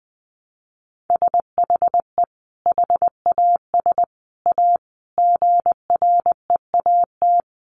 Morse code, heard as a keyed tone, about 20 wpm, 700 Hz. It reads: SHE HAS A GREAT
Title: Sentences from Top 1300 Words 20wpm